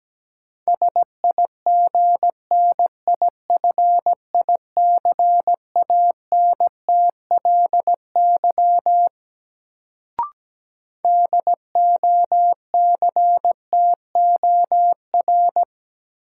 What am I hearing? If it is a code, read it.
Morse code, heard as a keyed tone, about 17 wpm, 700 Hz. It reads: SIGNIFICANTLY DOCTOR
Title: Sentences from Top 1500 Words - New Word Drill Alt - 17wpm